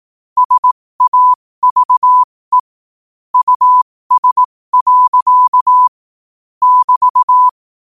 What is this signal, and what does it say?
Morse code, heard as a keyed tone, about 18 wpm, spaced as for 15 wpm, 1000 Hz. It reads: SAVE US. <BT>
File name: qotdfiles / 15wpm